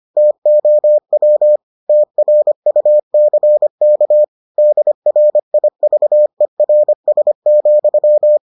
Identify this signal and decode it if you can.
Morse code, heard as a keyed tone, about 25 wpm, 600 Hz. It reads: TOW TRUCK DRIVERS,